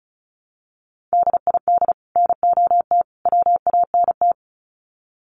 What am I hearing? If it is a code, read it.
Morse code, heard as a keyed tone, about 34 wpm, 700 Hz. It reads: DID NOT WANT
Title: Sets of 3 Words 35wpm